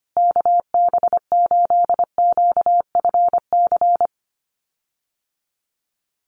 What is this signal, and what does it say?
Morse code, heard as a keyed tone, about 25 wpm, 700 Hz. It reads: KB8QFC
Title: USA Call Signs Repeated 2x 25wpm